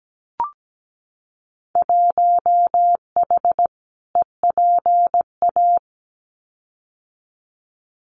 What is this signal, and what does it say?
Morse code, heard as a keyed tone, about 17 wpm, 700 Hz. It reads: 1H EPA
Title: Winter Field Day Exchanges 17wpm